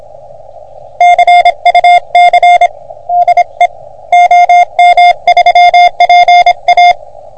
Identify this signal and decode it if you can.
Morse code, heard as a keyed tone, about 26 wpm, 720 Hz. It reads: CUC DE OM3PA
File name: OM3PA PILUP CUC 388